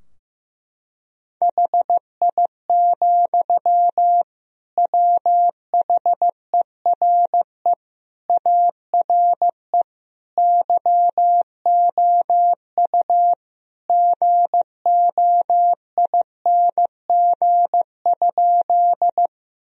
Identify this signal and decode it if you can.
Morse code, heard as a keyed tone, about 15 wpm, 700 Hz. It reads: HI, WHERE ARE YOU GOING?
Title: Sentences from Top 1000 Words - Review Part 3 - 15wpm